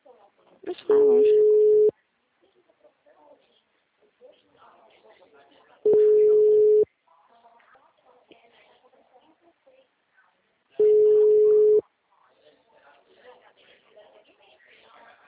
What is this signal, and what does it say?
Ringback tone (Europe, 425 Hz), with speech.
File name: out- -2009- - - .18624